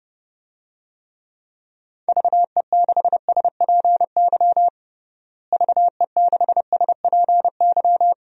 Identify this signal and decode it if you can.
Morse code, heard as a keyed tone, about 31 wpm, 700 Hz. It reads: VE6SPY VE6SPY
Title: DXCC-100 Call Signs - 6 Characters - Repeated 2x - 30wpm